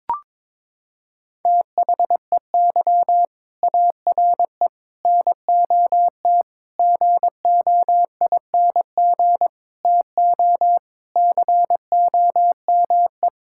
Morse code, heard as a keyed tone, about 22 wpm, 700 Hz. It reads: THEY ARE NOT GOING TO COME
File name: practice / 22wpm / Sentences with 4-Word Phrases - 22wpm